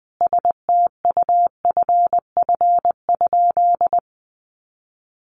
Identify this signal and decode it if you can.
Morse code, heard as a keyed tone, about 20 wpm, 700 Hz. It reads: STUFF?
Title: Sentences from Top 1400 Words 20wpm